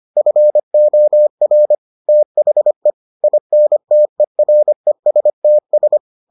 Morse code, heard as a keyed tone, about 25 wpm, 600 Hz. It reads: FOR THE INTERESTS